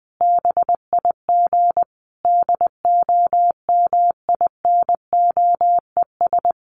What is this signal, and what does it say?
Morse code, heard as a keyed tone, about 20 wpm, 700 Hz. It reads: BIG DOMINOES
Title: A Sound Of Thunder 20wpm